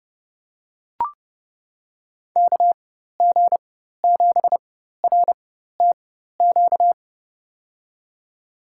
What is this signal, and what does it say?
Morse code, heard as a keyed tone, about 30 wpm, spaced as for 14 wpm, 700 Hz. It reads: KG7RTQ
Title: USA Call Signs - 6 Characters - 4x Character Spacing - 30wpm